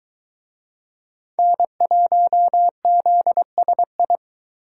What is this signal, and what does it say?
Morse code, heard as a keyed tone, about 24 wpm, 700 Hz. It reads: N1ZSI